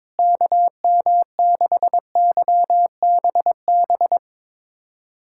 Morse code, heard as a keyed tone, about 21 wpm, 700 Hz. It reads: KM6YBB
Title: USA Callsign DE Callsign - 6 Characters - 22wpm